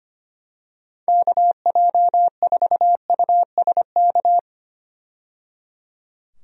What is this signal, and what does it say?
Morse code, heard as a keyed tone, about 25 wpm, 700 Hz. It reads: KJ4USK
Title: USA Call Signs - 6 Characters - 25wpm